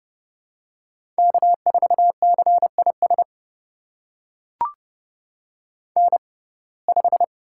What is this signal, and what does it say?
Morse code, heard as a keyed tone, about 30 wpm, spaced as for 22 wpm, 700 Hz. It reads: K4CIS N5
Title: USA Call Signs - 5 Characters - 6x Character Spacing - 30wpm